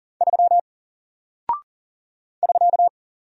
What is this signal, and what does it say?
Morse code, heard as a keyed tone, about 40 wpm, 700 Hz. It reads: 3 <SK>
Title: Single Character (Letter:Number:Punctuation:Prosign) Alt 40wpm